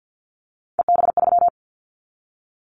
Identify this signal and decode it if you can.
Morse code, heard as a keyed tone, about 50 wpm, 700 Hz. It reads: EB3
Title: Callsign Prefixes for Top 25 DXCC - 50wpm